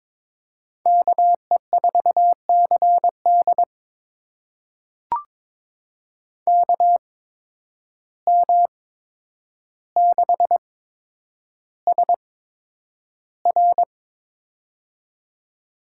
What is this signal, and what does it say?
Morse code, heard as a keyed tone, about 22 wpm, spaced as for 9 wpm, 700 Hz. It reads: KE4CD KM6SR
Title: USA Call Signs - 5 Characters - 8x Character Spacing - 22wpm